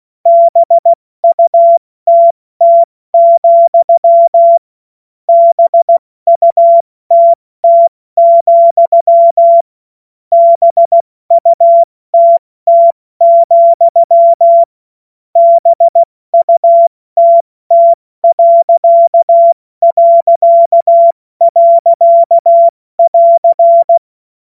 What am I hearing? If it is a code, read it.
Morse code, heard as a keyed tone, about 16 wpm, 680 Hz. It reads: BUTT, BUTT, BUTT, BUTT...<AR>
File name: Morsekurs / Witze / Witz 08. August 2023 70 F80